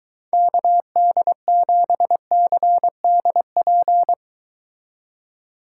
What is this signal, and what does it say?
Morse code, heard as a keyed tone, about 23 wpm, 700 Hz. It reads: KD7CDP